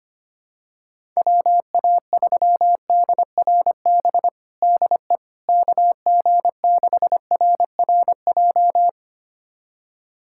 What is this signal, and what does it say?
Morse code, heard as a keyed tone, about 25 wpm, 700 Hz. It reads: WA3DRB DE KG6RRJ
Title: USA Callsign DE Callsign - 6 Characters - 25wpm